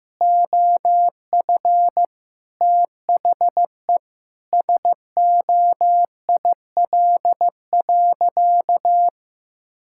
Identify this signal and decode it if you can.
Morse code, heard as a keyed tone, about 16 wpm, 700 Hz. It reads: OF THE SOIL.
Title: Sentences from Top 1000 Words - Review Part 5 - 15wpm